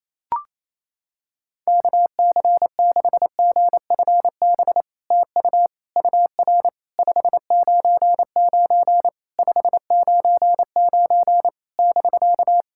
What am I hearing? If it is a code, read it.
Morse code, heard as a keyed tone, about 29 wpm, 700 Hz. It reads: KC6GFB TU UR 599 599 <BK>
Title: SOTA QSO Practice 28wpm